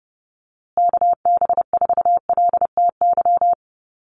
Morse code, heard as a keyed tone, about 30 wpm, 700 Hz. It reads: KB4LTY
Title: USA Call Signs Repeated 2x 30wpm